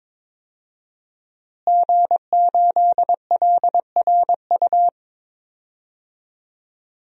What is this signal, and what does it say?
Morse code, heard as a keyed tone, about 22 wpm, 700 Hz. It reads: G8LRU
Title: DXCC-100 Call Signs - 5 Characters - Repeated 3x - 22wpm